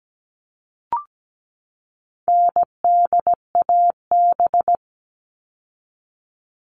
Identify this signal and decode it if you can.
Morse code, heard as a keyed tone, about 17 wpm, 700 Hz. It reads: NDAB
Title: All Cut Numbers - 4-Digit 17wpm